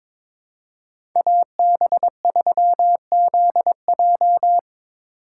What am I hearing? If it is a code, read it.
Morse code, heard as a keyed tone, about 22 wpm, 700 Hz. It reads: AB3ZJ
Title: USA Call Signs - 5 Characters - 2x Character Spacing - 22wpm